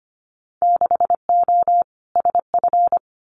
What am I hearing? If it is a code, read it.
Morse code, heard as a keyed tone, about 25 wpm, 700 Hz. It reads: 6O SF
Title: Winter Field Day Exchanges Encore 25wpm